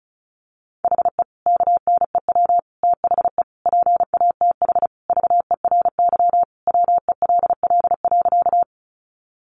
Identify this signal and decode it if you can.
Morse code, heard as a keyed tone, about 34 wpm, 700 Hz. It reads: HE KNEW THE PATH VERY WELL.
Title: Sentences from Top 1400 Words 35wpm